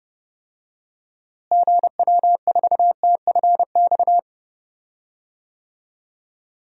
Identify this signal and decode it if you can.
Morse code, heard as a keyed tone, about 30 wpm, 700 Hz. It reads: GW4TFX
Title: DXCC-100 Call Signs - 6 Characters - Repeated 2x - 30wpm